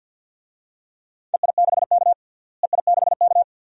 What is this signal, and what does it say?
Morse code, heard as a keyed tone, about 52 wpm, 700 Hz. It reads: EI6K EI6K
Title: DXCC-100 Call Signs - 4 Characters - Repeated 2x - 50wpm